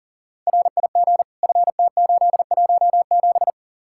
Morse code, heard as a keyed tone, about 48 wpm, 700 Hz. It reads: RIG FT817
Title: Speed-Racing Common QSO Phrases Encore - 40wpm